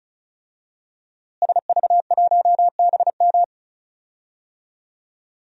Morse code, heard as a keyed tone, about 34 wpm, 700 Hz. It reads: SV1BM
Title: DXCC-100 Call Signs - 5 Characters - Repeated 3x - 35wpm